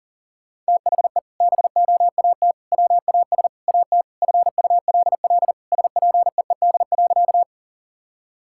Morse code, heard as a keyed tone, about 40 wpm, 700 Hz. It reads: THE BOAT WAS AT FULL SPEED.
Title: Sentences from Top 1000 Words - Review Part 5 - 40wpm